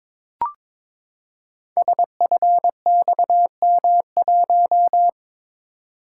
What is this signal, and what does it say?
Morse code, heard as a keyed tone, about 22 wpm, 700 Hz. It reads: SFXM1
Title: Code Groups - 5 Letter:Numbers - 22wpm